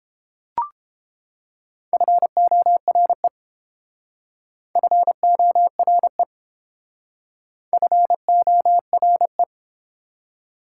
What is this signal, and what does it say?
Morse code, heard as a keyed tone, about 28 wpm, 700 Hz. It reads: FORE FORE FORE